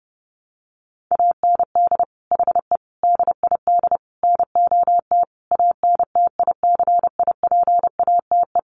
Morse code, heard as a keyed tone, about 28 wpm, 700 Hz. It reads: AND HE DID NOT ANTICIPATE
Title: Sentences from Top 1500 Words 30wpm